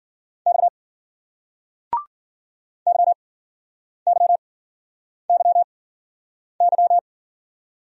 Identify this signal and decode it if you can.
Morse code, heard as a keyed tone, about 56 wpm, spaced as for 9 wpm, 700 Hz. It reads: X YYYY